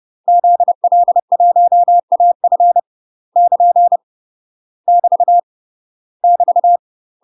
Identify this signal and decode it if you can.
Morse code, heard as a keyed tone, about 30 wpm, 700 Hz. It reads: ZL1AF <KN> <BT> <BT>
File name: UK MASTER QSOs TRAINING / 30WPM / UK MASTER QSOs 0021